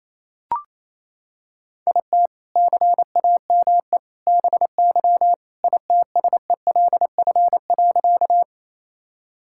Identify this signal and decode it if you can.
Morse code, heard as a keyed tone, about 27 wpm, 700 Hz. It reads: IT CAME BY ITSELF.
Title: Sentences from Top 1000 Words - Review Part 5 - 28wpm